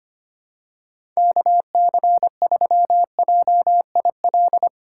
Morse code, heard as a keyed tone, about 25 wpm, 700 Hz. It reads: KC3JIL
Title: USA Call Signs - 6 Characters - 2x Character Spacing - 25wpm